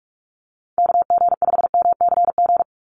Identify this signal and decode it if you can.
Morse code, heard as a keyed tone, about 46 wpm, 700 Hz. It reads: KG5MCG
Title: USA Call Signs 45wpm